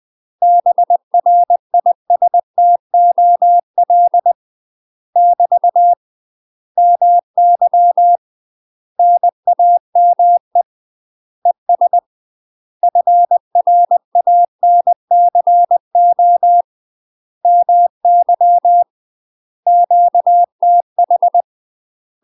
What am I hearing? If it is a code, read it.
Morse code, heard as a keyed tone, about 20 wpm, 700 Hz. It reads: BRISTOL <BT> MY NAME ES FRANCO MY QTH
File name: UK MASTER QSOs TRAINING/20WPM